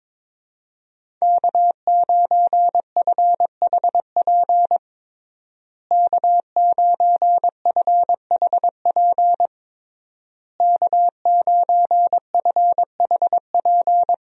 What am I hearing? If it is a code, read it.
Morse code, heard as a keyed tone, about 21 wpm, 700 Hz. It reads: K9FHP K9FHP K9FHP
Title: USA Call Signs - 5 Characters - Repeated 3x - 22wpm